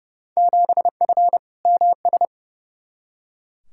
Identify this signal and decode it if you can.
Morse code, heard as a keyed tone, about 29 wpm, 700 Hz. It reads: 7F MS
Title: Field Day Exchanges Encore - 30wpm